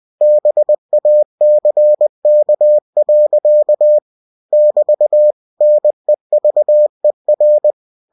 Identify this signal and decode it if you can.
Morse code, heard as a keyed tone, about 20 wpm, 600 Hz. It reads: BACK. <BT> NEVER
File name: cw-books / winnie-the-pooh / winnie-20wpm-page-0088